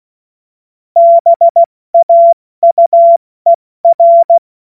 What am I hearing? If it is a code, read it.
Morse code, heard as a keyed tone, about 16 wpm, 680 Hz. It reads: BAUER
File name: Morsekurs / Witze / Witz 13 Mai 2025 70 F80